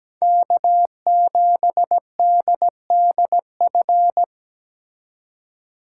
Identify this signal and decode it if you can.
Morse code, heard as a keyed tone, about 20 wpm, 700 Hz. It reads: K7DDF
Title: Speed Racing - USA 1x3 Call Signs 17wpm